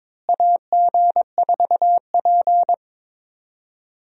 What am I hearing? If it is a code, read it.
Morse code, heard as a keyed tone, about 22 wpm, 700 Hz. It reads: AG4P
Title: USA Call Signs - 4 Characters - Repeated 3x - 22wpm